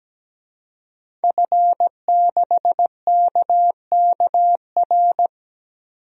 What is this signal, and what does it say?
Morse code, heard as a keyed tone, about 17 wpm, 700 Hz. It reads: F6KKR
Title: DXCC-100 Call Signs - 5 Characters - 17wpm